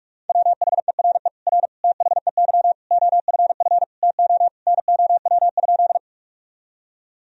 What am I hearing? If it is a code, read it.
Morse code, heard as a keyed tone, about 46 wpm, 700 Hz. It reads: WHERE R THEY OFF TO NOW?
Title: Sentences from Top 200 Words and Abbreviations 45wpm